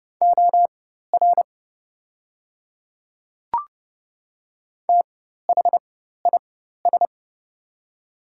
Morse code, heard as a keyed tone, about 30 wpm, spaced as for 14 wpm, 690 Hz. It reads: OR THIS
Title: 4x Character Spacing - Top 100 Words - 30wpm